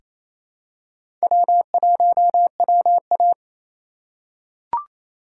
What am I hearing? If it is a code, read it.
Morse code, heard as a keyed tone, about 28 wpm, 700 Hz. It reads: W1WA
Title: USA Call Signs - 4 Characters - 2x Character Spacing - 28wpm